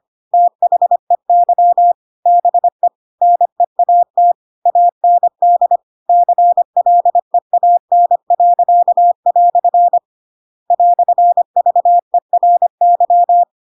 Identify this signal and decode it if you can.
Morse code, heard as a keyed tone, about 25 wpm, 700 Hz. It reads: THEY BE NEAT AND CLEAN." "VERY